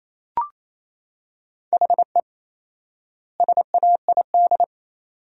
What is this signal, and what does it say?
Morse code, heard as a keyed tone, about 28 wpm, 700 Hz. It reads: HE SAID
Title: 4x Word-Spacing - Sentences from Top 200 Words - 28wpm